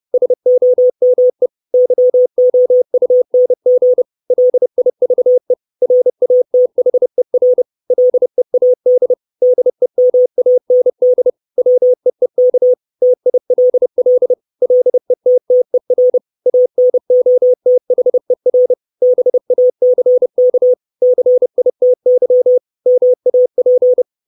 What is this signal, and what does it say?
Morse code, heard as a keyed tone, about 30 wpm, 500 Hz. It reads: SOME YOUNG LIVE RATHER LEAD DEMAND WEEK TILL LETTER ANOTHER BACK CITY MAP
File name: 632 Words 30WPM